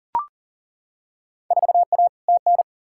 Morse code, heard as a keyed tone, about 38 wpm, 700 Hz. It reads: 4A TN
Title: Field Day Exchanges - 40wpm